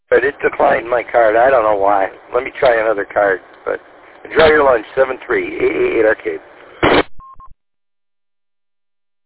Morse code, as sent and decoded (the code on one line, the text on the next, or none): none